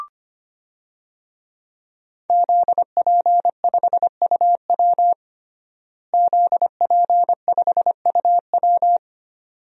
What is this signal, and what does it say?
Morse code, heard as a keyed tone, about 25 wpm, 700 Hz. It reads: ZP5UW ZP5UW
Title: DXCC-100 Call Signs - 5 Characters - Repeated 2x - 25wpm